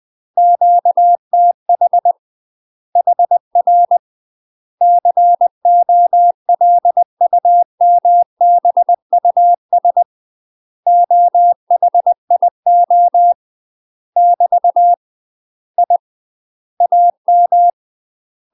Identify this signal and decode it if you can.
Morse code, heard as a keyed tone, about 20 wpm, 700 Hz. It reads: QTH HR COLUMBUS OHIO <BT> I AM
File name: UK MASTER QSOs TRAINING / 20WPM / UK MASTER QSOs 0026